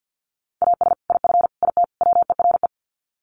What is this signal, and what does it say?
Morse code, heard as a keyed tone, about 50 wpm, 700 Hz. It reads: AS IF IT WERE
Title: Sets of 4 Words 50wpm